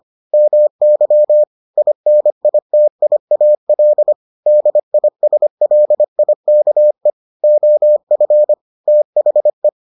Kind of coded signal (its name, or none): Morse code